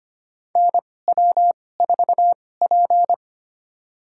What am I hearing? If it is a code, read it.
Morse code, heard as a keyed tone, about 25 wpm, spaced as for 18 wpm, 700 Hz. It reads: NW4P
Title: USA Call Signs - 4 Characters - 2x Character Spacing - 25wpm